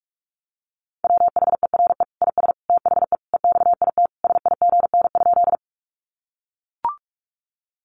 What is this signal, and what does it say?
Morse code, heard as a keyed tone, about 44 wpm, 700 Hz. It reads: WHERE IS THE EXIT SIGN?
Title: Sentences from Top 1100 Words 45wpm